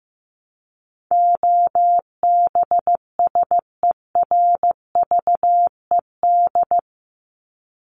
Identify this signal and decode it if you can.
Morse code, heard as a keyed tone, about 14 wpm, 700 Hz. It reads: OBSERVED
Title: ED Suffix - Top 100 Words with Base Words 15wpm